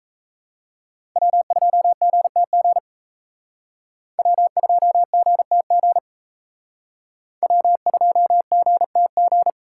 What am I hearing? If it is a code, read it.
Morse code, heard as a keyed tone, about 32 wpm, 700 Hz. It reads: W2GTG W2GTG W2GTG
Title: Speed Racing - USA 1x3 Call Signs 28wpm